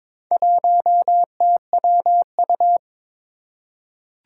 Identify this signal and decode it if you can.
Morse code, heard as a keyed tone, about 22 wpm, 700 Hz. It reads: 1TWU
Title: Four Challenging Letter-Number Combinations 22wpm